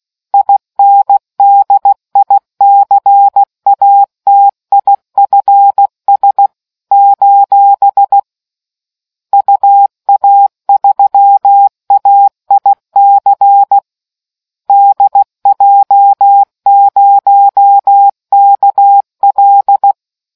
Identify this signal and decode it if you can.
Morse code, heard as a keyed tone, about 16 wpm, 800 Hz. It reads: INDICATIFS : UA3AIC DJ0KL